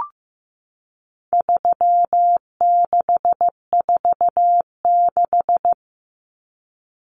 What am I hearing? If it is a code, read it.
Morse code, heard as a keyed tone, about 15 wpm, 700 Hz. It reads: 3646
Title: Common Cut Numbers - 4-Digit 15wpm